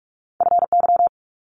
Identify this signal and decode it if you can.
Morse code, heard as a keyed tone, about 46 wpm, 700 Hz. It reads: FY
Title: Multi-letter Phonemes Quick Fire 45wpm